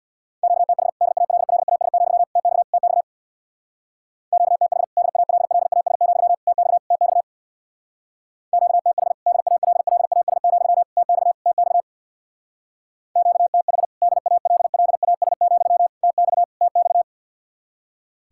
Morse code, heard as a keyed tone, about 59 wpm, 700 Hz. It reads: QTH DALLAS, TX TX QTH DALLAS, TX TX QTH DALLAS, TX TX QTH DALLAS, TX TX
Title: Speed-Racing Common QSO Phrases Encore - 50wpm